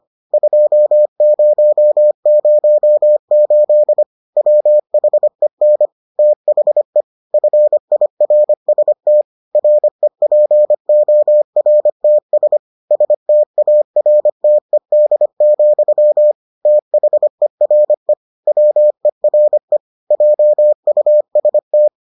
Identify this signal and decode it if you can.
Morse code, heard as a keyed tone, about 25 wpm, 600 Hz. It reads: 2008 WHEN THE FIRST REPORTS STARTED, THERE WERE JUST